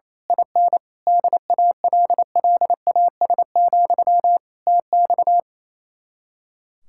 Morse code, heard as a keyed tone, about 26 wpm, 700 Hz. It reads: IN DALLAS, TX
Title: Speed-Racing Common QSO Phrases - 28wpm